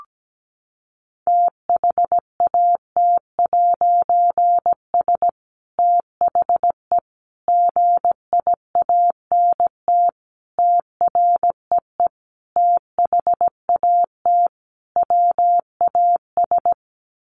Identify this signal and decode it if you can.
Morse code, heard as a keyed tone, about 17 wpm, 700 Hz. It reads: THAT'S THE GIANT TREE THAT WAS
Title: A Sound Of Thunder 17wpm